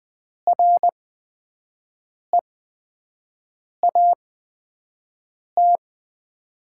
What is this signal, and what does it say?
Morse code, heard as a keyed tone, about 20 wpm, spaced as for 5 wpm, 700 Hz. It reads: REAT
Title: 8x Character Spacing - Top 300 Words - 20wpm